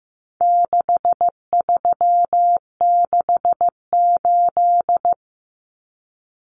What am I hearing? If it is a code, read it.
Morse code, heard as a keyed tone, about 15 wpm, 700 Hz. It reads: 6368
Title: Common Cut Numbers - 4-Digit 15wpm